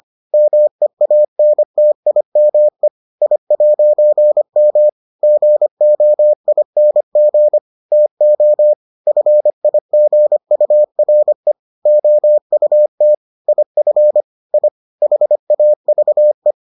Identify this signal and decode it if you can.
Morse code, heard as a keyed tone, about 25 wpm, 600 Hz. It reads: MEANTIME I'M GOING TO FIGURE OUT IF I HAVE